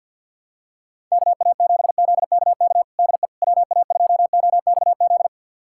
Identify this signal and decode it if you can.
Morse code, heard as a keyed tone, about 46 wpm, 700 Hz. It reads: KA7ZKK DE WA2OXZ
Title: USA Callsign DE Callsign - 6 Characters - 50wpm